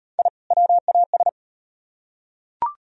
Morse code, heard as a keyed tone, about 38 wpm, 700 Hz. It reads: I WAS